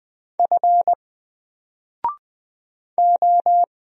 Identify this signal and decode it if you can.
Morse code, heard as a keyed tone, about 20 wpm, 700 Hz. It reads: F O
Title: Single Character (Letter:Number:Punctuation:Prosign) 20wpm